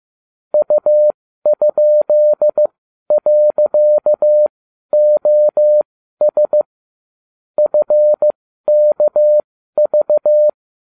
Morse code, heard as a keyed tone, about 15 wpm, spaced as for 12 wpm, 600 Hz. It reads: U?.OS FKV